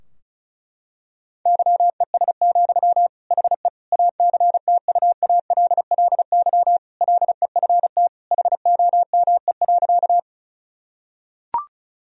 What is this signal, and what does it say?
Morse code, heard as a keyed tone, about 36 wpm, 700 Hz. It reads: YES, HE ACTUALLY LEFT HOME.
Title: Sentences from Top 1000 Words - Review Part 4 - 35wpm